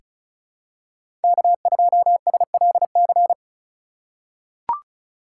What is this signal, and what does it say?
Morse code, heard as a keyed tone, about 34 wpm, 700 Hz. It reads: K2SLC
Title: USA Call Signs - 5 Characters - 4x Character Spacing - 35wpm